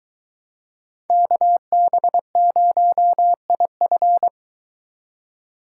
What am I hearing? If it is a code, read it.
Morse code, heard as a keyed tone, about 23 wpm, 700 Hz. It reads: KB0IF